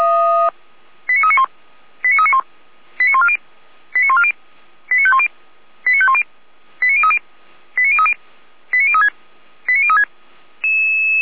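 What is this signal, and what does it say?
Five-tone selective call (ZVEI), heard as quick runs of five tones, tones of about 80 ms. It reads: 89391 89391 89140 89140 89510 89510 89030 89030 89036 89036
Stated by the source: Five-tone ZVEI selective call received on an SDR.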